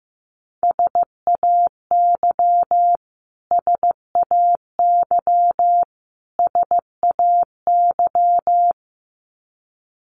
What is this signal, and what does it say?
Morse code, heard as a keyed tone, about 15 wpm, 700 Hz. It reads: SAY SAY SAY
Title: Top 500 Words Repeated 15wpm